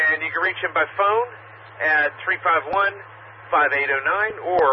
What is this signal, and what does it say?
Emergency Alert System (SAME) tones, heard as harsh buzzing bursts, with speech. It is heard from 0.2 to 4.1 s.